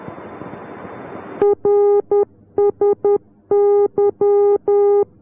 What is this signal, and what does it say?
Morse code, heard as a keyed tone, about 10 wpm, 410 Hz. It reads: RSY